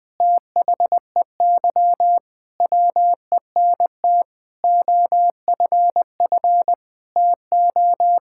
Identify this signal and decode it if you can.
Morse code, heard as a keyed tone, about 21 wpm, 700 Hz. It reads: THEY WENT OFF TO
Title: Sentences from Top 1600 Words 20wpm